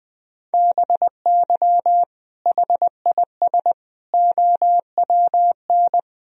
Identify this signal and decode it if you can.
Morse code, heard as a keyed tone, about 20 wpm, 700 Hz. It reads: BY HIS OWN
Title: Sentences from Top 1000 Words - Review Part 1 - 20wpm